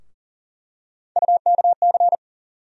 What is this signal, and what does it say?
Morse code, heard as a keyed tone, about 39 wpm, 700 Hz. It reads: UKC
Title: USA Call Sign Postfixes - 3 Letters 40wpm